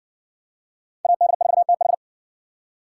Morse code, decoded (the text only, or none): AD4TH